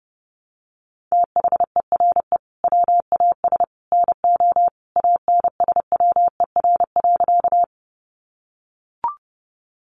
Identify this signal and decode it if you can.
Morse code, heard as a keyed tone, about 30 wpm, 700 Hz. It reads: THERE WAS NO ANSWER.
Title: Sentences from Top 300 Words 30wpm